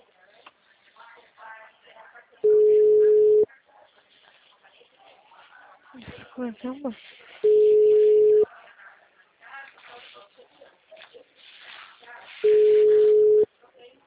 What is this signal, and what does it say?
Ringback tone (Europe, 425 Hz), with speech.